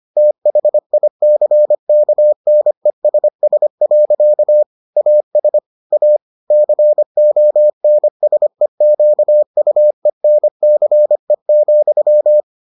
Morse code, heard as a keyed tone, about 25 wpm, 600 Hz. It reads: THICKNESS. AS A CONSEQUENCE,